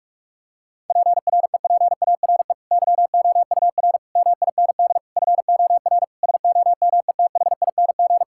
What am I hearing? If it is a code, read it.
Morse code, heard as a keyed tone, about 44 wpm, 700 Hz. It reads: PREPARE YOUR MIND FOR SOMETHING
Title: Sentences from Top 1000 Words - Review Part 6 - 45wpm